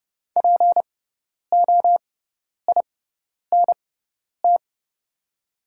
Morse code, heard as a keyed tone, about 30 wpm, spaced as for 10 wpm, 700 Hz. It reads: POINT
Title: 6x Character Spacing - Top 300 Words - 30wpm